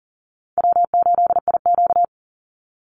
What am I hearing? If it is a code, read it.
Morse code, heard as a keyed tone, about 40 wpm, 700 Hz. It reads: W8IQ
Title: Easy Call Signs - 40wpm